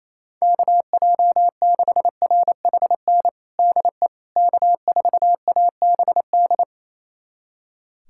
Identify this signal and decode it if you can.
Morse code, heard as a keyed tone, about 28 wpm, 700 Hz. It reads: KJ6RHN DE K4ABD
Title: USA Callsign DE Callsign 28wpm